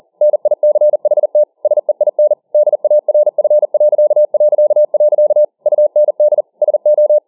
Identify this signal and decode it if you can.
Morse code, heard as a keyed tone, about 40 wpm, 600 Hz. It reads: NICHT SEIN DARF... UND SO